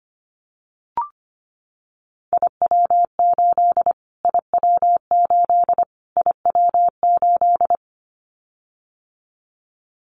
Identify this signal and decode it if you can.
Morse code, heard as a keyed tone, about 25 wpm, 700 Hz. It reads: IW8 IW8 IW8
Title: Callsign Prefixes for Top 25 DXCC - Repeated 3x - 25wpm